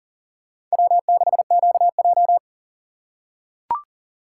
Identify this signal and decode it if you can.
Morse code, heard as a keyed tone, about 40 wpm, 700 Hz. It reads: W6QJ
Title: USA Call Signs - 4 Characters - 4x Character Spacing - 40wpm